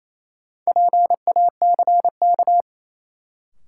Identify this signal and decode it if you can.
Morse code, heard as a keyed tone, about 27 wpm, 700 Hz. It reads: PACK
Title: Sentences from Top 1300 Words - New Word Drill - 28wpm